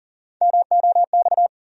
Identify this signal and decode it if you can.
Morse code, heard as a keyed tone, about 40 wpm, 700 Hz. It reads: MOX
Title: USA Call Sign Postfixes - 3 Letters 40wpm